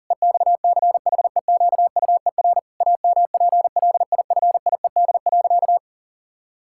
Morse code, heard as a keyed tone, about 40 wpm, 700 Hz. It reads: EXCHEQUER AMPLIFIED.
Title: Pangram Challenge v2 - 40wpm